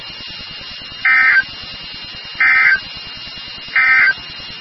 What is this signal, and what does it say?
Emergency Alert System (SAME) tones, heard as harsh buzzing bursts. It is heard from 0.9 to 4.2 s.
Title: NOAA Weather Radio WXK36